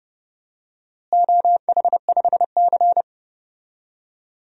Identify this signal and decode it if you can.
Morse code, heard as a keyed tone, about 30 wpm, 700 Hz. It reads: OH5C
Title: DXCC-100 Call Signs - 4 Characters - Repeated 3x - 30wpm